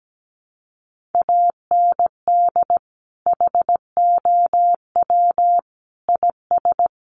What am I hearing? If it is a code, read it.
Morse code, heard as a keyed tone, about 17 wpm, 700 Hz. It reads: AND HOW IS
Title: Sentences from Top 1400 Words 17wpm